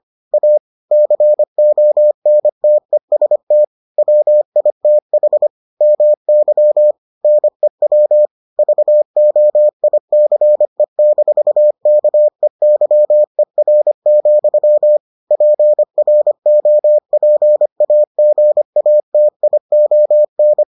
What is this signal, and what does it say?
Morse code, heard as a keyed tone, about 25 wpm, 600 Hz. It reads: A CONTEST WITH MY NEW VOICE-KEYER, PROPAGATION